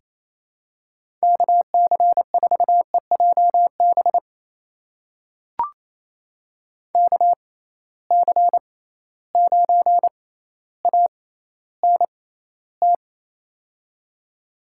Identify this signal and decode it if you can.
Morse code, heard as a keyed tone, about 28 wpm, spaced as for 14 wpm, 700 Hz. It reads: KC4EJB KC9ANT
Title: USA Call Signs - 6x Character Spacing - 28wpm